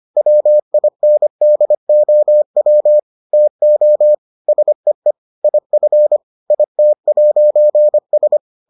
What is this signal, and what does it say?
Morse code, heard as a keyed tone, about 25 wpm, 600 Hz. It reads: WINDOW TO SEE IF IT'S